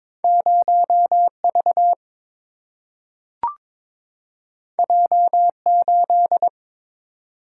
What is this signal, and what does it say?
Morse code, heard as a keyed tone, about 22 wpm, 700 Hz. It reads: 0 V J 8
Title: Single-Letter Single-Number 22wpm